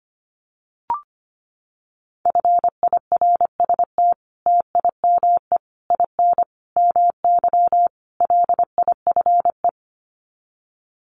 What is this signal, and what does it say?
Morse code, heard as a keyed tone, about 25 wpm, 700 Hz. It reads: FIRST TIME IN MY LIFE
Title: Sets of 5 Words Encore2 25wpm